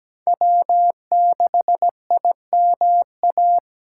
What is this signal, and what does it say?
Morse code, heard as a keyed tone, about 20 wpm, 700 Hz. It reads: W6IMA
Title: Speed Racing - USA 1x3 Call Signs 17wpm